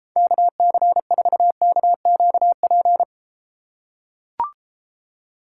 Morse code, heard as a keyed tone, about 32 wpm, 700 Hz. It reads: KC4KQP